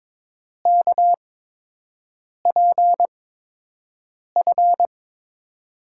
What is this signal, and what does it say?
Morse code, heard as a keyed tone, about 22 wpm, spaced as for 6 wpm, 700 Hz. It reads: KPF
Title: USA Call Signs - 6 Characters - 8x Character Spacing - 22wpm